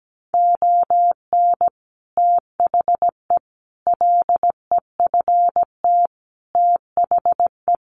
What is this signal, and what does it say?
Morse code, heard as a keyed tone, about 17 wpm, 700 Hz. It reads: ON THE LEFT THE
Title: Sentences from Top 700 Words 17wpm